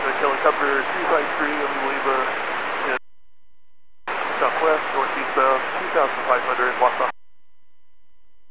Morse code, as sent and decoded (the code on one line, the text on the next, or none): none